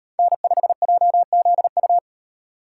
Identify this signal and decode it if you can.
Morse code, heard as a keyed tone, about 38 wpm, 700 Hz. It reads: N5JZU